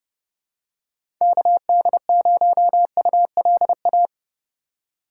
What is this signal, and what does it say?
Morse code, heard as a keyed tone, about 31 wpm, 700 Hz. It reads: KD0ULA
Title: USA Call Signs - 6 Characters - 3x Character Spacing - 30wpm